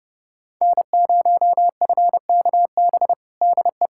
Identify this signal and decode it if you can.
Morse code, heard as a keyed tone, about 30 wpm, 700 Hz. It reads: N0FKB DE
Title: USA Callsign DE Callsign - 5 Characters - 30wpm